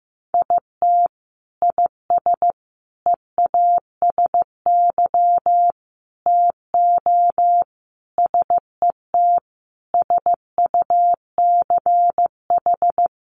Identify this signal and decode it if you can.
Morse code, heard as a keyed tone, about 15 wpm, 700 Hz. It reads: IT IS EASY TO SET SUCH
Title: Sentences from Top 1300 Words 15wpm